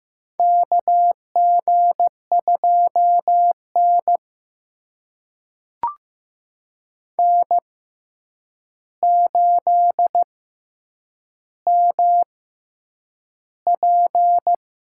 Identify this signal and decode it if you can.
Morse code, heard as a keyed tone, about 15 wpm, spaced as for 8 wpm, 700 Hz. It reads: KG2N N8MP
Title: USA Call Signs - 4 Characters - 6x Character Spacing - 15wpm